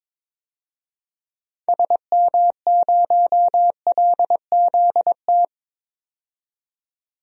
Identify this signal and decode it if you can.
Morse code, heard as a keyed tone, about 21 wpm, 700 Hz. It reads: SM0LZT
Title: DXCC-100 Call Signs - 6 Characters - 22wpm